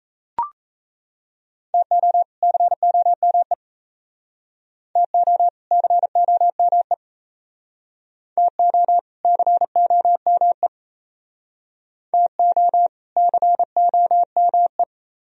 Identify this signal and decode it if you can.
Morse code, heard as a keyed tone, about 35 wpm, 700 Hz. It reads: TO COME TO COME TO COME TO COME
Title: Speed-Racing - Sets of 2 Words 28wpm